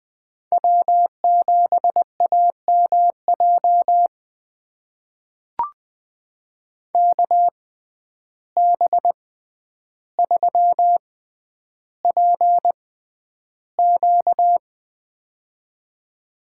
Morse code, heard as a keyed tone, about 20 wpm, spaced as for 10 wpm, 700 Hz. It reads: W7AMJ KB3PQ
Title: USA Call Signs - 5 Characters - 6x Character Spacing - 20wpm